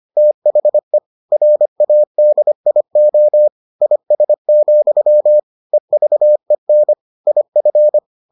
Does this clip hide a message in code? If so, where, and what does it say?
Yes: Morse code from 0.2 to 8.0 s, heard as a keyed tone, about 25 wpm, 600 Hz. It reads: THE RADIO IS, EVEN IF